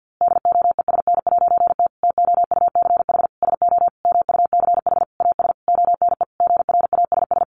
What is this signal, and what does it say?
Morse code, heard as a keyed tone, about 49 wpm, 700 Hz. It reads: DOESN'T TOUCH SO MUCH AS ONE GRASS
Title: A Sound Of Thunder 50wpm